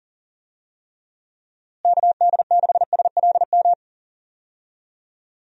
Morse code, heard as a keyed tone, about 40 wpm, 700 Hz. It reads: KD6SLM
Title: USA Call Signs - 6 Characters - Repeated 2x - 40wpm